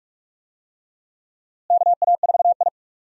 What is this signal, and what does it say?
Morse code, heard as a keyed tone, about 46 wpm, 700 Hz. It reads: KA4I
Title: USA Call Signs - 4 Characters - Repeated 3x - 45wpm